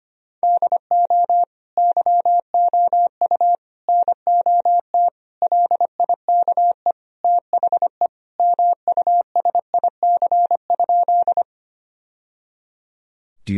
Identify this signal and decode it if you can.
Morse code, heard as a keyed tone, about 25 wpm, 700 Hz. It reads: DO YOU NOT LIKE THE MUSIC?
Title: Sentences from Top 1000 Words - Review Part 1 - 25wpm